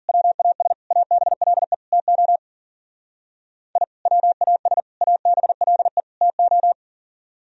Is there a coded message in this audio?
Yes: Morse code from 0.1 to 6.7 s, heard as a keyed tone, about 51 wpm, 700 Hz. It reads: WAS ABLE TO I WAS ABLE TO